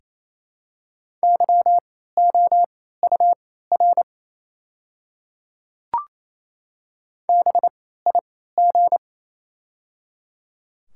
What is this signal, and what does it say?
Morse code, heard as a keyed tone, about 28 wpm, spaced as for 16 wpm, 700 Hz. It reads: YOUR BIG
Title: 3x Character Spacing - Top 200 Words - 28wpm